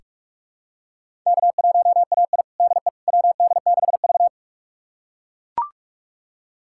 Morse code, heard as a keyed tone, about 44 wpm, 700 Hz. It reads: K1AI DE WD6V
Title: USA Callsign DE Callsign - 4 Characters - 45wpm